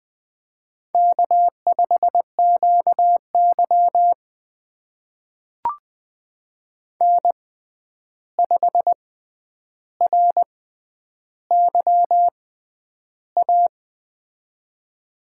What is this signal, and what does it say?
Morse code, heard as a keyed tone, about 20 wpm, spaced as for 10 wpm, 700 Hz. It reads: K5QY N5RYA
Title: USA Call Signs - 6x Character Spacing - 20wpm